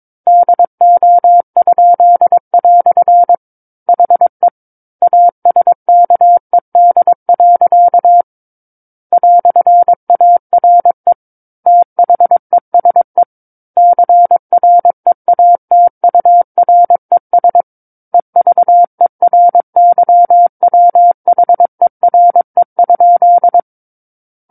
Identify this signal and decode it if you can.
Morse code, heard as a keyed tone, about 22 wpm, 700 Hz. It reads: DO?" HE ASKED. "ARE THESE CREATURES EVERYWHERE?
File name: Book One / 13-Chapter 13 - 22 WPM